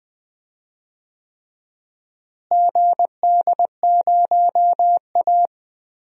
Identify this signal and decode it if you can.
Morse code, heard as a keyed tone, about 20 wpm, 700 Hz. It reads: GD0A
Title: DXCC-100 Call Signs - 4 Characters - Repeated 3x - 20wpm